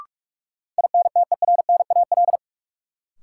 Morse code, heard as a keyed tone, about 47 wpm, 700 Hz. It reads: INTERNAL
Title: Sentences from Top 1400 Words - New Word Drill Alt - 45wpm